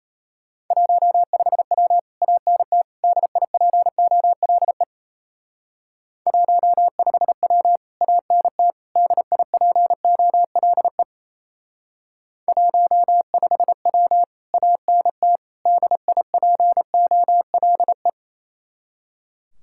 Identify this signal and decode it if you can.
Morse code, heard as a keyed tone, about 36 wpm, 700 Hz. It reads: 15W ANT DIPOLE 15W ANT DIPOLE 15W ANT DIPOLE